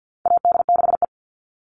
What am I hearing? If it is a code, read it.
Morse code, heard as a keyed tone, about 50 wpm, 700 Hz. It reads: AD6E
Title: Call Signs Repeated 2x - 50wpm